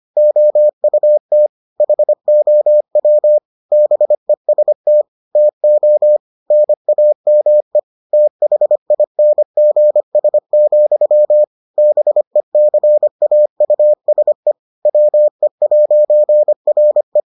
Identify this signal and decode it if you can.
Morse code, heard as a keyed tone, about 25 wpm, 600 Hz. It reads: OUT HOW BEST TO NAME THINGS, BECAUSE WE'RE